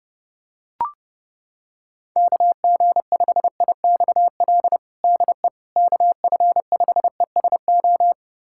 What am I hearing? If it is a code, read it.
Morse code, heard as a keyed tone, about 30 wpm, 690 Hz. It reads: KG5IXL DE KF5ESO
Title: USA Callsign DE Callsign - 6 Characters - 30wpm